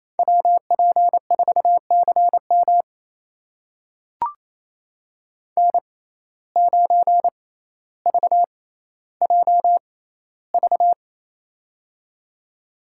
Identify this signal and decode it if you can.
Morse code, heard as a keyed tone, about 28 wpm, spaced as for 14 wpm, 700 Hz. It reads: WP4CM N9VJV
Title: USA Call Signs - 5 Characters - 6x Character Spacing - 28wpm